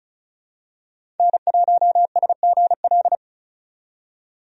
Morse code, heard as a keyed tone, about 36 wpm, 700 Hz. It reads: N1SGL
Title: USA Call Signs - 5 Characters - 35wpm